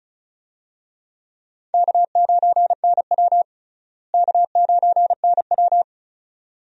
Morse code, heard as a keyed tone, about 36 wpm, 700 Hz. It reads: K9NW K9NW
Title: USA Call Signs - 4 Characters - Repeated 2x - 35wpm